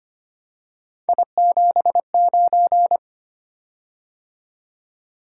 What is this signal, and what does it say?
Morse code, heard as a keyed tone, about 25 wpm, 700 Hz. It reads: I79
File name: Three Letter-Number Combinations - 25wpm